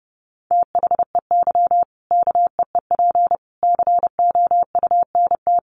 Morse code, heard as a keyed tone, about 30 wpm, 700 Hz. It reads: THEY KEEP COUNT
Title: Sentences from Top 1500 Words 30wpm